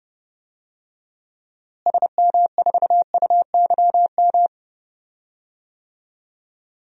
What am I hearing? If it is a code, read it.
Morse code, heard as a keyed tone, about 30 wpm, 700 Hz. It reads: SM4UYM
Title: DXCC-100 Call Signs - 6 Characters - Repeated 2x - 30wpm